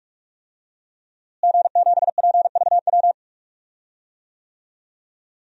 Morse code, heard as a keyed tone, about 46 wpm, 700 Hz. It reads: G7PVW
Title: DXCC-100 Call Signs - 5 Characters - Repeated 3x - 45wpm